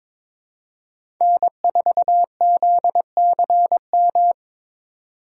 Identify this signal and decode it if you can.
Morse code, heard as a keyed tone, about 22 wpm, 700 Hz. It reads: N4ZCM
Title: USA Call Signs - 5 Characters - 2x Character Spacing - 22wpm